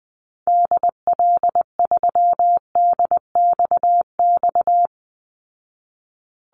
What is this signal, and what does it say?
Morse code, heard as a keyed tone, about 20 wpm, 700 Hz. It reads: DL3DXX
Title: Call Signs Repeated 3x - 20wpm